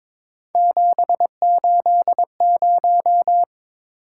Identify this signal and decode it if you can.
Morse code, heard as a keyed tone, about 22 wpm, 700 Hz. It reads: 780
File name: practice / 22wpm / Three Letter-Number Combinations - 22wpm